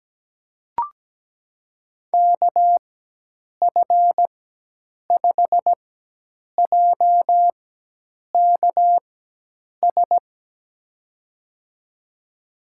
Morse code, heard as a keyed tone, about 17 wpm, spaced as for 8 wpm, 700 Hz. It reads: KF5JKS
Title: USA Call Signs - 6 Characters - 4x Character Spacing - 17wpm